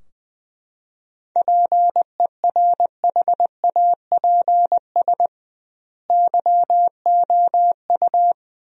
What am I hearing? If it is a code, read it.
Morse code, heard as a keyed tone, about 20 wpm, 700 Hz. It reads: PERHAPS YOU
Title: 2x Word-Spacing - Sentences from Top 500 Words - 20wpm